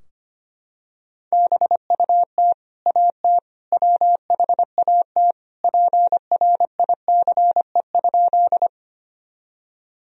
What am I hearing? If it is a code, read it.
Morse code, heard as a keyed tone, about 25 wpm, 700 Hz. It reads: BUT AT WHAT PRICE?
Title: Sentences from Top 1000 Words - Review Part 4 - 25wpm